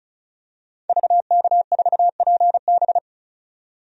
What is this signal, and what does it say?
Morse code, heard as a keyed tone, about 36 wpm, 700 Hz. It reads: VK4PB
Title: DXCC-100 Call Signs - 5 Characters - 35wpm